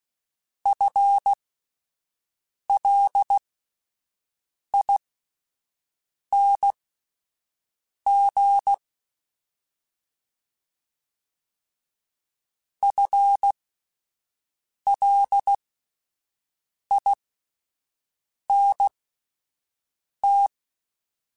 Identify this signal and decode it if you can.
Morse code, heard as a keyed tone, about 16 wpm, spaced as for 5 wpm, 800 Hz. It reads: FLING FLINT
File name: Spaced / FL Runs sp mp3s / FL Runs sp @10